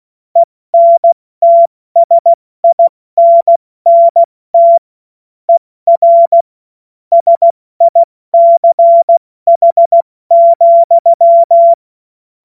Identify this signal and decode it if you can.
Morse code, heard as a keyed tone, about 16 wpm, spaced as for 14 wpm, 680 Hz. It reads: ENTSINNT ER SICH,
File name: Morsekurs / Witze / Witz 31. Mai 2022 70 F80